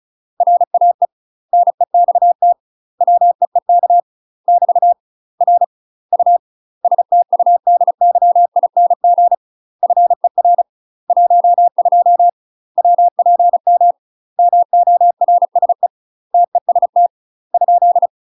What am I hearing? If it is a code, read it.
Morse code, heard as a keyed tone, about 36 wpm, 700 Hz. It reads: RAE NEXT WEEK <BT> R U STUDYING FER 12 WPM MORSE TEST ?
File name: UK MASTER QSOs TRAINING / 35WPM / UK MASTER QSOs 0028